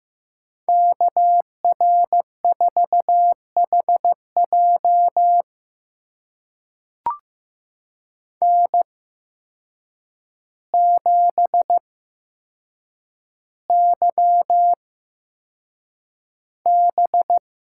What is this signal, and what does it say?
Morse code, heard as a keyed tone, about 15 wpm, spaced as for 7 wpm, 700 Hz. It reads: KR4HJ N7YB
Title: USA Call Signs - 5 Characters - 8x Character Spacing - 15wpm